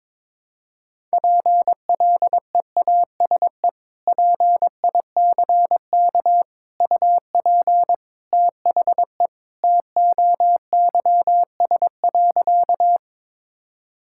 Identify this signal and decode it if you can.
Morse code, heard as a keyed tone, about 22 wpm, 700 Hz. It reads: PLEASE PICK UP THE TOYS.
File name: practice / 22wpm / Sentences with 2-Word Phrases - 22wpm